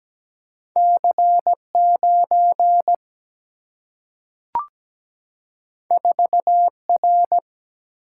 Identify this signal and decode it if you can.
Morse code, heard as a keyed tone, about 17 wpm, 700 Hz. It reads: C9 4R
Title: Single-Letter Single-Number 17wpm